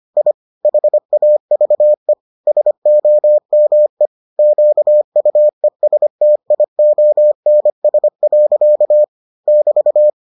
Morse code, heard as a keyed tone, about 25 wpm, 600 Hz. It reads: I HAVE SOME QUESTIONS. <BT>